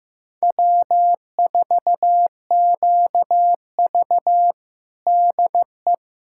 Morse code, heard as a keyed tone, about 15 wpm, 700 Hz. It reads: W4QV DE
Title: USA Callsign DE Callsign - 4 Characters - 15wpm